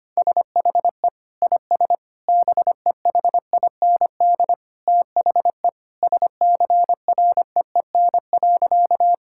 Morse code, heard as a keyed tone, about 25 wpm, 700 Hz. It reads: SHE IS BEHIND THE SCREEN.
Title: Sentences from Top 1000 Words - Review Part 6 - 25wpm